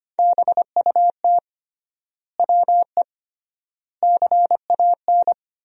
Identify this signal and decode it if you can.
Morse code, heard as a keyed tone, about 25 wpm, 700 Hz. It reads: BUT WE CAN
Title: 3x Word-Spacing - Sentences from Top 100 Words - 25wpm